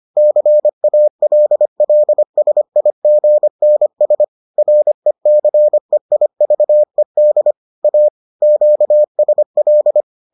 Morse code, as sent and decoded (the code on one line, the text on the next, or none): -.-. .- .-.. .-.. ... .. --. -. ... / .-. -.-. .. ...- . -.. / .- / --.- ... .-..
CALLSIGNS RECEIVED A QSL